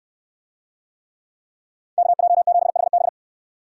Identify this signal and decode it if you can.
Morse code, heard as a keyed tone, about 69 wpm, 700 Hz. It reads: KJ7HB